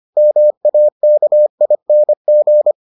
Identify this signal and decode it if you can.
Morse code, heard as a keyed tone, about 25 wpm, 600 Hz. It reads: MAKING